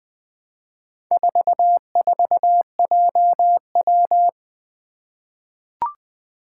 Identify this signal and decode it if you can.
Morse code, heard as a keyed tone, about 21 wpm, 700 Hz. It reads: 44JW
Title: Four Challenging Letter-Number Combinations 20wpm